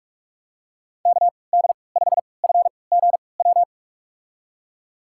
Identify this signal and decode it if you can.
Morse code, heard as a keyed tone, about 42 wpm, spaced as for 25 wpm, 700 Hz. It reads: KD5FGW
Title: USA Call Signs - 6 Characters - 3x Character Spacing - 45wpm